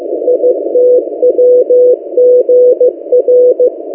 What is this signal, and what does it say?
Morse code, heard as a keyed tone, about 15 wpm, 500 Hz. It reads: ITWGR
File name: M08a- -1355utc-8096khz